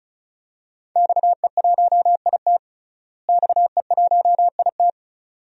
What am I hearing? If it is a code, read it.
Morse code, heard as a keyed tone, about 36 wpm, 700 Hz. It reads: XE1IT XE1IT
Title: DXCC-100 Call Signs - 5 Characters - Repeated 2x - 35wpm